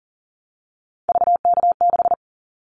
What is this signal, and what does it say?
Morse code, heard as a keyed tone, about 38 wpm, 700 Hz. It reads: VK6
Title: Callsign Prefixes for Top 50 DXCC - 40wpm